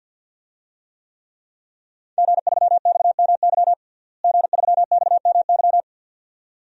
Morse code, heard as a keyed tone, about 52 wpm, 700 Hz. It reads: G3XMY G3XMY
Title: DXCC-100 Call Signs - 5 Characters - Repeated 2x - 50wpm